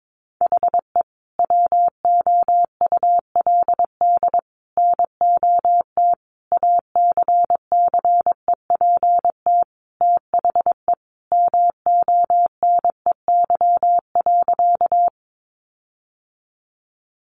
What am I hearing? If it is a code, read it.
Morse code, heard as a keyed tone, about 22 wpm, 700 Hz. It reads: HE WOULD NOT ACCEPT THE MONEY.
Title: Sentences from Top 1300 Words 22wpm